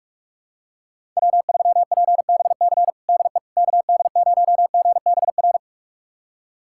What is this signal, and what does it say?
Morse code, heard as a keyed tone, about 46 wpm, 700 Hz. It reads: W3PBC DE KD0GBR
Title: USA Callsign DE Callsign 45wpm